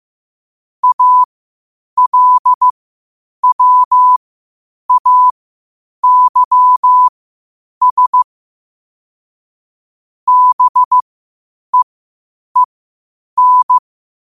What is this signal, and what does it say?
Morse code, heard as a keyed tone, about 15 wpm, spaced as for 8 wpm, 1000 Hz. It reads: ALWAYS BEEN
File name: qotdfiles / 8wpm